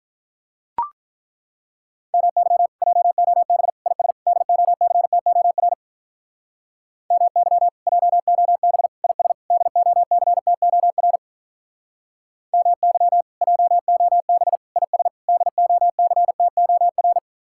Morse code, heard as a keyed tone, about 43 wpm, 700 Hz. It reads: MY JOB IS DOCTOR MY JOB IS DOCTOR MY JOB IS DOCTOR
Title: Speed-Racing Common QSO Phrases - 35wpm